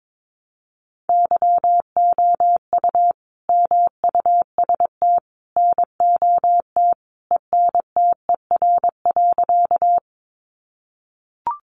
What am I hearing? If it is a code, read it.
Morse code, heard as a keyed tone, about 22 wpm, 700 Hz. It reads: YOU MUST NOT ENTER.
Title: Sentences from Top 1300 Words 22wpm